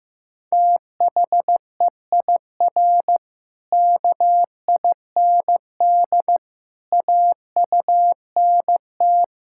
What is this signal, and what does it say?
Morse code, heard as a keyed tone, about 15 wpm, 700 Hz. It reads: THEIR KIND AUNT
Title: Pangram Challenge v2 - 15wpm